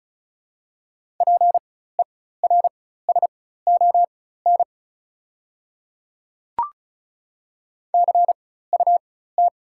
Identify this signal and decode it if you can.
Morse code, heard as a keyed tone, about 36 wpm, spaced as for 16 wpm, 690 Hz. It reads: PERSON CUT
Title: 4x Character Spacing - Top 500 Words - 35wpm